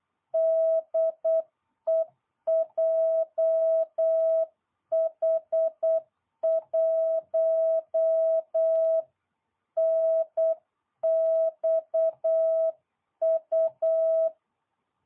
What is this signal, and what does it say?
Morse code, heard as a keyed tone, about 8 wpm, 660 Hz. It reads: DE JH1NXU